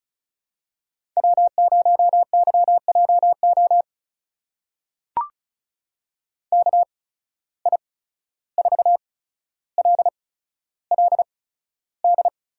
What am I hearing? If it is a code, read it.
Morse code, heard as a keyed tone, about 36 wpm, spaced as for 14 wpm, 700 Hz. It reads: W0YJO KI4LLD
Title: USA Call Signs - 8x Character Spacing - 35wpm